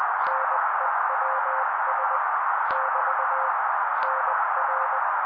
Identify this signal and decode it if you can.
Morse code, heard as a keyed tone, about 20 wpm, 530 Hz. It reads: NEWS <BT> NR
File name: daily / 18wpm noise